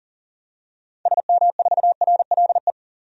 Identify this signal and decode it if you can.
Morse code, heard as a keyed tone, about 40 wpm, 700 Hz. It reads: SM4RLE
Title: DXCC-100 Call Signs - 6 Characters - 40wpm